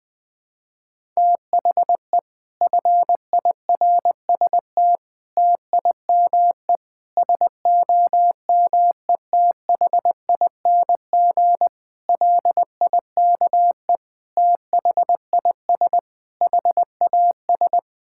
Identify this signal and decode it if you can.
Morse code, heard as a keyed tone, about 20 wpm, 700 Hz. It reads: THE FIRST TIME SOMETHING LIKE THIS HAS
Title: Sets of 8 Words - 20wpm